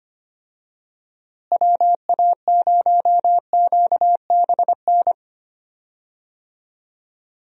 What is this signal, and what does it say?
Morse code, heard as a keyed tone, about 25 wpm, 700 Hz. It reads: WA0QBN
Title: USA Call Signs - 6 Characters - Repeated 3x - 25wpm